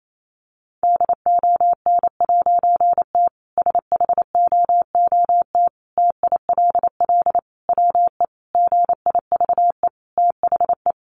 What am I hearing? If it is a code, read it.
Morse code, heard as a keyed tone, about 28 wpm, 700 Hz. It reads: DON'T SHOOT TILL WE GIVE THE
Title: A Sound Of Thunder 28wpm